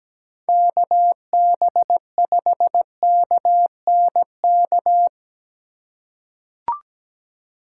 Morse code, heard as a keyed tone, about 17 wpm, 700 Hz. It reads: KB5KNK
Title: USA Call Signs - 6 Characters - 2x Character Spacing - 17wpm